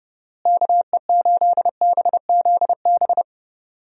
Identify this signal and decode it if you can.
Morse code, heard as a keyed tone, about 30 wpm, 700 Hz. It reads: KE8BZB
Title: USA Call Signs - 6 Characters - Repeated 3x - 30wpm